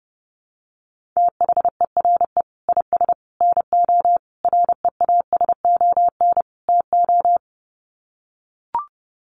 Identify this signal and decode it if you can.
Morse code, heard as a keyed tone, about 30 wpm, 700 Hz. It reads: THERE IS NO REASON TO
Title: Sets of 5 Words Encore 30wpm